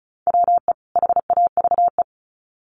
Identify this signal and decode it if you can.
Morse code, heard as a keyed tone, about 34 wpm, 700 Hz. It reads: WE HAVE
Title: Sets of 2 Words 35wpm